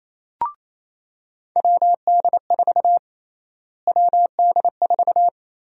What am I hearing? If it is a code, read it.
Morse code, heard as a keyed tone, about 27 wpm, 700 Hz. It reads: WD4 WD4
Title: USA Call Sign Prefixes - Repeated 2x - 28wpm